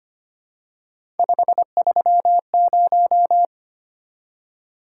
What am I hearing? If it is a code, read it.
Morse code, heard as a keyed tone, about 25 wpm, 700 Hz. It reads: 530
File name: Three Letter-Number Combinations - 25wpm